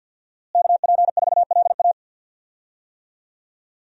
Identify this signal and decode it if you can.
Morse code, heard as a keyed tone, about 52 wpm, 700 Hz. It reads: KP4LA
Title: USA Call Signs - 5 Characters - Repeated 2x - 50wpm